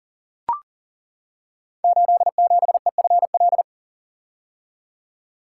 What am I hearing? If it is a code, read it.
Morse code, heard as a keyed tone, about 43 wpm, 700 Hz. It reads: 87EFL
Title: Code Groups - 5 Letter:Numbers - 40wpm